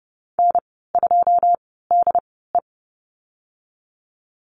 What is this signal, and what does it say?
Morse code, heard as a keyed tone, about 30 wpm, spaced as for 17 wpm, 700 Hz. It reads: N2DE
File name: Lesson 14g - Number 2 - Call Signs 30wpm